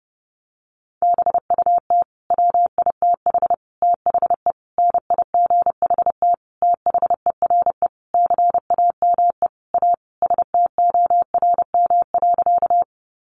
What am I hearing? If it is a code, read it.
Morse code, heard as a keyed tone, about 30 wpm, 700 Hz. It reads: BUT WITH THE NIGHT THERE CAME A STORM.
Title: Sentences from Top 1500 Words 30wpm